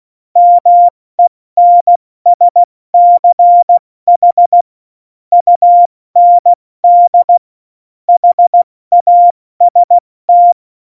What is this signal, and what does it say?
Morse code, heard as a keyed tone, about 16 wpm, 700 Hz. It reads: MENSCH UND HAST
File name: Morsekurs / Witze / Witz 26. Januar 2021, 70 F80